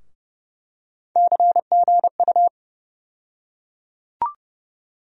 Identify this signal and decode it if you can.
Morse code, heard as a keyed tone, about 30 wpm, 700 Hz. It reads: CGU
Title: USA Call Sign Postfixes - 3 Letters 30wpm